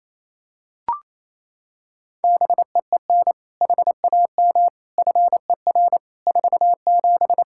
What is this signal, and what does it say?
Morse code, heard as a keyed tone, about 28 wpm, 700 Hz. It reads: BEEN HAM FER 47
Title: Practice QSO - Exchange 3 - More Info - 28wpm